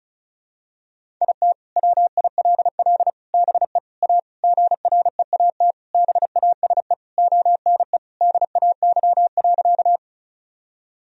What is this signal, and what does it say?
Morse code, heard as a keyed tone, about 36 wpm, 700 Hz. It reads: IT WILL BE A GREAT BASE ONE DAY.
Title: Sentences from Top 1000 Words - Review Part 1 - 35wpm